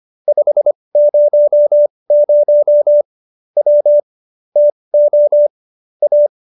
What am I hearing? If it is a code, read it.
Morse code, heard as a keyed tone, about 25 wpm, spaced as for 20 wpm, 590 Hz. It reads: 500 W TO A